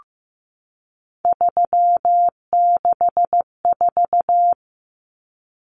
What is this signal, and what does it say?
Morse code, heard as a keyed tone, about 15 wpm, 700 Hz. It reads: 364
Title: Numbers - 3-Digit 15wpm